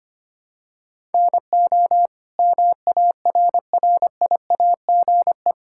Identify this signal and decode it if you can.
Morse code, heard as a keyed tone, about 25 wpm, 700 Hz. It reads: NO MARRIAGE
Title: Sentences from Top 1000 Words - Review Part 5 - 25wpm